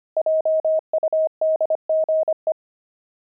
Morse code, heard as a keyed tone, about 25 wpm, 630 Hz. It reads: JUDGE
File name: pangram-fs-25-cs-25-watches